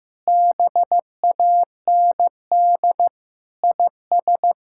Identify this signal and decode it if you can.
Morse code, heard as a keyed tone, about 15 wpm, 700 Hz. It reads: BAND IS
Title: Sentences from Top 1600 Words 15wpm